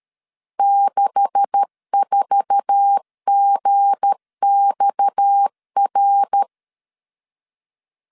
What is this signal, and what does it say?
Morse code, heard as a keyed tone, about 13 wpm, 800 Hz. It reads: 64GXR
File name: CW / DET / 1 9 1500C S A 12wpm 3800